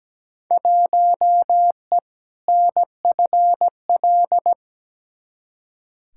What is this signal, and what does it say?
Morse code, heard as a keyed tone, about 17 wpm, 700 Hz. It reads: 1E NFL
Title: Field Day Exchanges - 17wpm